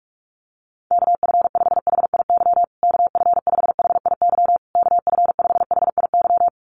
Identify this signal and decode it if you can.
Morse code, heard as a keyed tone, about 44 wpm, 700 Hz. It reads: KF5HIY KF5HIY KF5HIY
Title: USA Call Signs Repeated 3x 45wpm